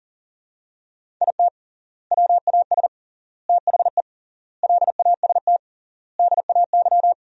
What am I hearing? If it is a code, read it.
Morse code, heard as a keyed tone, about 40 wpm, 700 Hz. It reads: IT WAS THE LAST DAY
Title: 3x Word-Spacing - Sentences from Top 300 Words - 40wpm